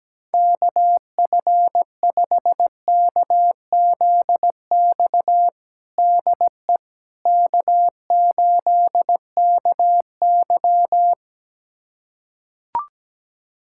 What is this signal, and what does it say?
Morse code, heard as a keyed tone, about 17 wpm, 700 Hz. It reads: KF5KZX DE K8KY
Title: USA Callsign DE Callsign 17wpm